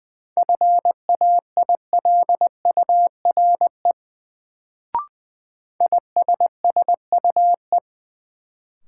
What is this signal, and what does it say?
Morse code, heard as a keyed tone, about 20 wpm, 700 Hz. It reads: FAILURE ISSUE
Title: Sentences from Top 900 Words - New Word Drill - 20wpm